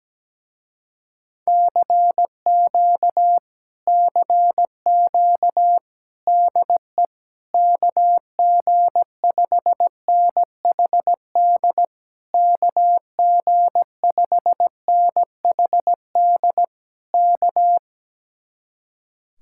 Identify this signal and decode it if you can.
Morse code, heard as a keyed tone, about 17 wpm, 700 Hz. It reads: CQ CQ DE KG5NHD KG5NHD K
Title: Speed-Racing Common QSO Phrases Encore - 17wpm